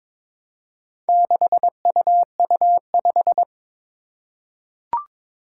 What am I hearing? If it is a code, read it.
Morse code, heard as a keyed tone, about 22 wpm, 700 Hz. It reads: 6UU5
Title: Four Challenging Letter-Number Combinations 22wpm